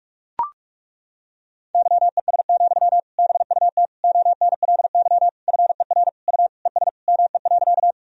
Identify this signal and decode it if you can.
Morse code, heard as a keyed tone, about 41 wpm, 700 Hz. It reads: YES, BUT ONLY FER U ES ME.
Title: Sentences from Top 500 Words and Abbreviations 45wpm